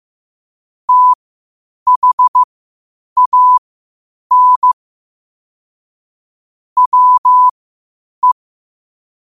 Morse code, heard as a keyed tone, about 15 wpm, spaced as for 8 wpm, 1000 Hz. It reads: THAN WE